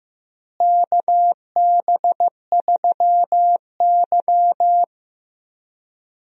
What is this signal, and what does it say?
Morse code, heard as a keyed tone, about 15 wpm, 700 Hz. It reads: KB3Y
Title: USA Call Signs - 4 Characters - Repeated 3x - 15wpm